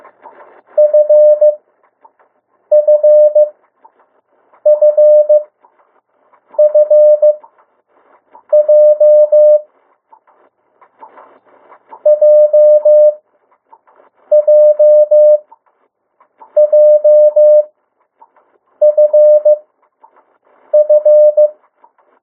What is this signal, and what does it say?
Morse code, heard as a keyed tone, about 15 wpm, spaced as for 6 wpm, 600 Hz. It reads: FFFFJ JJJFF
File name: urok 1